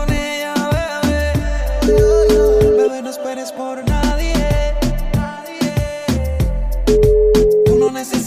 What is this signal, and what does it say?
Ringback tone (Europe, 425 Hz), with noise.